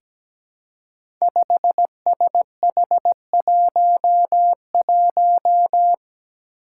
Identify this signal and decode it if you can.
Morse code, heard as a keyed tone, about 17 wpm, 700 Hz. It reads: 5SH11
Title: Five Challenging Letter-Number Combinations 17wpm